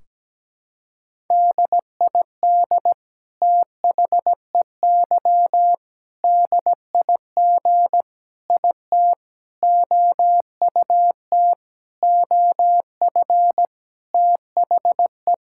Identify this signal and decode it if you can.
Morse code, heard as a keyed tone, about 17 wpm, 700 Hz. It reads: DID THEY DIG IT OUT OF THE
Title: Sentences from Top 1600 Words 17wpm